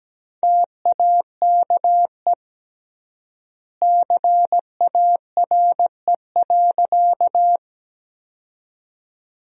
Morse code, heard as a keyed tone, about 17 wpm, 700 Hz. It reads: TAKE CARE.
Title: 3x Word-Spacing - Sentences from Top 300 Words - 17wpm